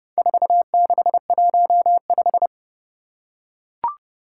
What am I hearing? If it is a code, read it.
Morse code, heard as a keyed tone, about 29 wpm, 700 Hz. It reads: 4615
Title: Four Challenging Letter-Number Combinations 30wpm